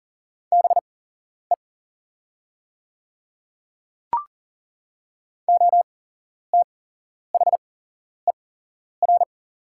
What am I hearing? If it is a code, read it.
Morse code, heard as a keyed tone, about 40 wpm, spaced as for 11 wpm, 700 Hz. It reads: BE OTHER
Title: 8x Character Spacing - Top 100 Words - 40wpm